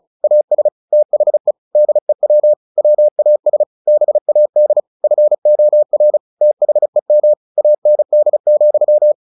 Morse code, heard as a keyed tone, about 35 wpm, 600 Hz. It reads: AS THE DEW WAS BAD FOR THEM AND,
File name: the old man and the sea / the old man and the sea 1 35wpm